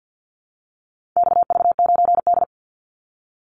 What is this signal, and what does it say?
Morse code, heard as a keyed tone, about 50 wpm, 700 Hz. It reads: XV9D
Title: Call Signs - 50wpm